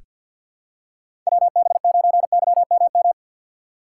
Morse code, heard as a keyed tone, about 46 wpm, 700 Hz. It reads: WB9YMM
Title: USA Call Signs - 6 Characters - 3x Character Spacing - 50wpm